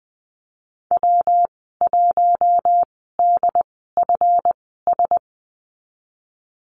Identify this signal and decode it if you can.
Morse code, heard as a keyed tone, about 20 wpm, spaced as for 15 wpm, 700 Hz. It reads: W1DFS
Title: 173 Lesson 28g - Number 9 - Call Signs 20wpm